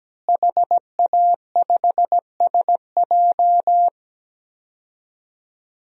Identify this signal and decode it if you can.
Morse code, heard as a keyed tone, about 17 wpm, 700 Hz. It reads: HA5SJ
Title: DXCC-100 Call Signs - 5 Characters - Repeated 3x - 17wpm